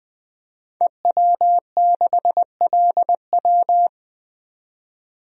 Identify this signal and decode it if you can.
Morse code, heard as a keyed tone, about 21 wpm, 700 Hz. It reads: EW6LW
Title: DXCC-100 Call Signs - 5 Characters - 20wpm